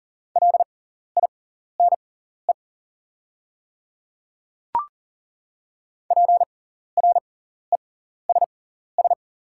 Morse code, heard as a keyed tone, about 40 wpm, spaced as for 14 wpm, 700 Hz. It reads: LINE PRESS
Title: 6x Character Spacing - Top 300 Words - 40wpm